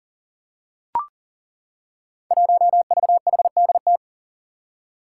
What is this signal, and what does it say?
Morse code, heard as a keyed tone, about 37 wpm, 700 Hz. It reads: 1VHDT
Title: Five Challenging Letter-Number Combinations 40wpm